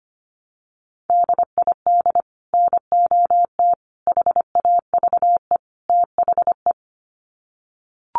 Morse code, heard as a keyed tone, about 25 wpm, 700 Hz. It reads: DID NOT HAVE THE
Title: Sets of 4 Words 25wpm